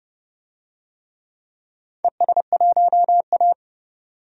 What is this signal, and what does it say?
Morse code, heard as a keyed tone, about 31 wpm, 700 Hz. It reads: ES1A
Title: DXCC-100 Call Signs - 4 Characters - 30wpm